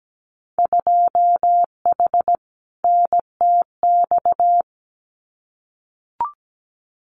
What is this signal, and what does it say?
Morse code, heard as a keyed tone, about 17 wpm, 700 Hz. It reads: 2H NTX
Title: Winter Field Day Exchanges Encore 17wpm